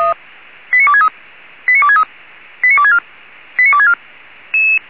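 Five-tone selective call (ZVEI), heard as quick runs of five tones, tones of about 70 ms. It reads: 89263 89263 89264 89264
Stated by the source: Five-tone ZVEI selective call 89273 received by SDR.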